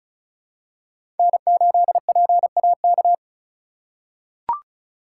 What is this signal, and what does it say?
Morse code, heard as a keyed tone, about 36 wpm, 700 Hz. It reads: N8PAK
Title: USA Call Signs - 5 Characters - 2x Character Spacing - 35wpm